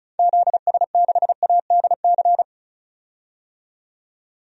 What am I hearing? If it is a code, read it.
Morse code, heard as a keyed tone, about 36 wpm, 700 Hz. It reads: ZS6ADC
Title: DXCC-100 Call Signs - 6 Characters - 35wpm